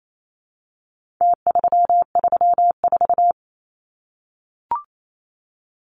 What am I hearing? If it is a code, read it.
Morse code, heard as a keyed tone, about 28 wpm, 700 Hz. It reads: T334
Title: Numbers - Contest Style - 4-Digit 28wpm